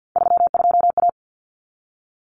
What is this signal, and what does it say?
Morse code, heard as a keyed tone, about 50 wpm, 700 Hz. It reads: 32A
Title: Common Cut Numbers - 3-Digit 50wpm